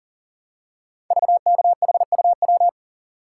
Morse code, heard as a keyed tone, about 41 wpm, 700 Hz. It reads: VKHUW
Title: Code Groups - 5 Letters - 40wpm